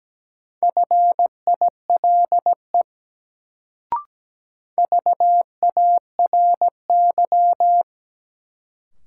Morse code, heard as a keyed tone, about 17 wpm, 700 Hz. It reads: FILE VARY
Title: Sentences from Top 1200 Words - New Word Drill Alt - 17wpm